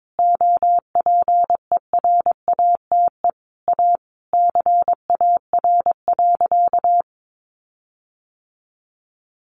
Morse code, heard as a keyed tone, about 22 wpm, 700 Hz. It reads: OPERATE A CAR.
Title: Sentences from Top 1500 Words 22wpm